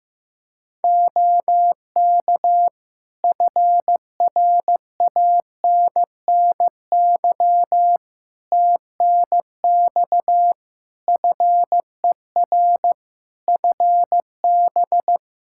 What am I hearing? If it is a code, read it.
Morse code, heard as a keyed tone, about 15 wpm, 700 Hz. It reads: OK FRANNY TNX FER FB
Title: Practice QSO - Exchange 4 - Ending - 15wpm